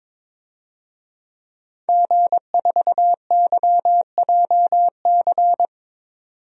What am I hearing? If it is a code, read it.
Morse code, heard as a keyed tone, about 21 wpm, 700 Hz. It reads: G4YJC
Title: DXCC-100 Call Signs - 5 Characters - 22wpm